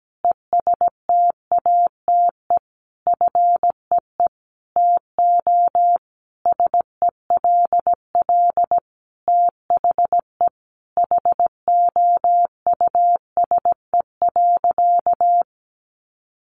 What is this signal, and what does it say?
Morse code, heard as a keyed tone, about 17 wpm, 700 Hz. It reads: ESTATE FEE TO SELL THE HOUSE.
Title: Sentences from Top 1500 Words 17wpm